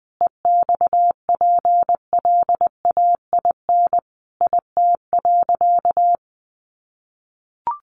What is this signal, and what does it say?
Morse code, heard as a keyed tone, about 20 wpm, 700 Hz. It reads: EXPLAIN IT.
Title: Sentences from Top 1100 Words 20wpm